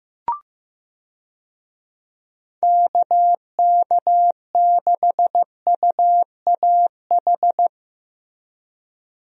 Morse code, heard as a keyed tone, about 15 wpm, 700 Hz. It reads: KK6UAH
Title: USA Call Signs - 6 Characters - Repeated 3x - 15wpm